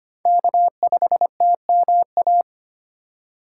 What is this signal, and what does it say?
Morse code, heard as a keyed tone, about 25 wpm, 700 Hz. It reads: K5TMA
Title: USA Call Signs - 2x Character Spacing - 25wpm